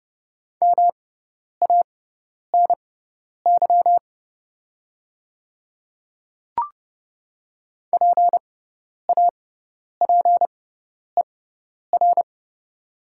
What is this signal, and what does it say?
Morse code, heard as a keyed tone, about 30 wpm, spaced as for 10 wpm, 700 Hz. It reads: MANY PAPER
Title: 6x Character Spacing - Top 300 Words - 30wpm